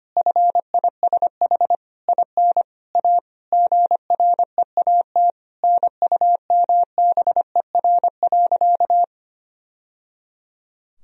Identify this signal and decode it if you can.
Morse code, heard as a keyed tone, about 25 wpm, 700 Hz. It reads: FISH IN A GREAT NUMBER.
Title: Sentences from Top 1000 Words - Review Part 1 - 25wpm